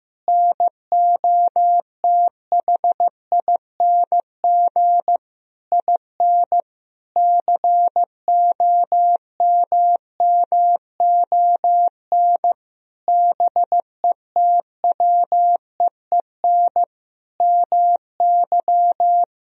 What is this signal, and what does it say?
Morse code, heard as a keyed tone, about 15 wpm, 700 Hz. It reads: NOTHING IN COMMON BETWEEN MY
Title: Sentences from Top 1000 Words - Review Part 3 - 15wpm